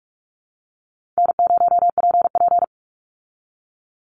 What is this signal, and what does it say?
Morse code, heard as a keyed tone, about 44 wpm, 700 Hz. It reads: N0PP
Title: Call Signs Repeated 2x - 45wpm